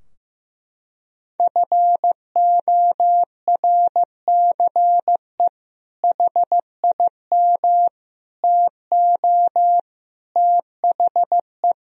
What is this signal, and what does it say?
Morse code, heard as a keyed tone, about 15 wpm, 700 Hz. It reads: FORCE HIM TO THE
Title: Sentences from Top 1000 Words - Review Part 3 - 15wpm